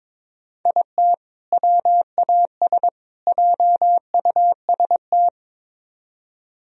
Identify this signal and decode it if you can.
Morse code, heard as a keyed tone, about 24 wpm, 700 Hz. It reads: IT WAS JUST
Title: Speed-Racing - Sets of 3 Words 22wpm